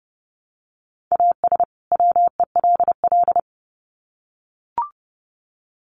Morse code, heard as a keyed tone, about 30 wpm, 700 Hz. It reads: AS WELL
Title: Sets of 2 Words 30wpm